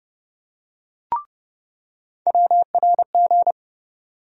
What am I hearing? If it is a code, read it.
Morse code, heard as a keyed tone, about 30 wpm, 700 Hz. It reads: WRG
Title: USA Call Sign Postfixes - 3 Letters 30wpm